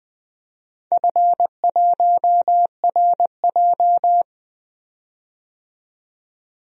Morse code, heard as a keyed tone, about 20 wpm, 700 Hz. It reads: F1RJ
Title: DXCC-100 Call Signs - 4 Characters - 20wpm